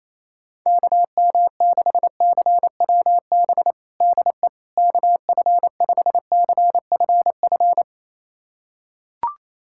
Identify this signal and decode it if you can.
Morse code, heard as a keyed tone, about 28 wpm, 700 Hz. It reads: KM6CWB DE KF5CFF
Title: USA Callsign DE Callsign 28wpm